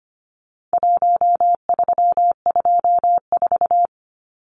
Morse code, heard as a keyed tone, about 25 wpm, 700 Hz. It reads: 1324
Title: Numbers - Contest Style - 4-Digit 25wpm